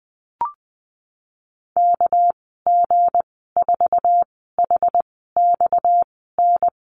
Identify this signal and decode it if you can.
Morse code, heard as a keyed tone, about 20 wpm, 700 Hz. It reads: K G 4 H X N
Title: 238 Lesson 38g - Letter q - Call Signs 20wpm